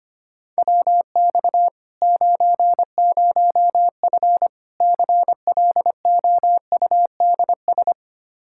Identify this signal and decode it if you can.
Morse code, heard as a keyed tone, about 25 wpm, 700 Hz. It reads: WX 90F CLOUDS
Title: Practice QSO - Exchange 2 - Info - 25wpm